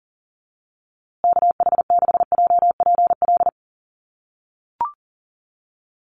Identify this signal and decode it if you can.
Morse code, heard as a keyed tone, about 40 wpm, 700 Hz. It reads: KH6JPL
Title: USA Call Signs Repeated 3x 40wpm